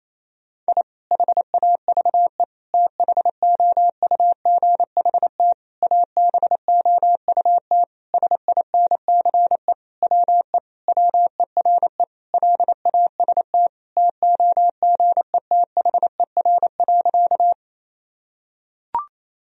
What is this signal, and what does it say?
Morse code, heard as a keyed tone, about 28 wpm, 700 Hz. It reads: I HAVE THOUGHT ABOUT SINCE WE WERE LAST TOGETHER.
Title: Sentences from Top 1000 Words - Review Part 1 - 28wpm